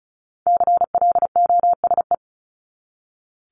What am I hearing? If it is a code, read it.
Morse code, heard as a keyed tone, about 34 wpm, 700 Hz. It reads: CLOSE
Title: LY Suffix - Top 100 Words with Base Words 35wpm